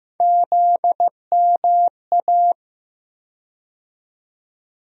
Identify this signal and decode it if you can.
Morse code, heard as a keyed tone, about 15 wpm, 700 Hz. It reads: ZMA
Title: USA Call Sign Postfixes - 3 Letters 15wpm